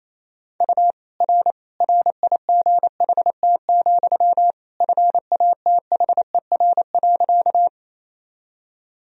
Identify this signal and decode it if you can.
Morse code, heard as a keyed tone, about 27 wpm, 700 Hz. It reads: U R RIGHT, FATHER.
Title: Sentences from Top 300 Words and Abbreviations 28wpm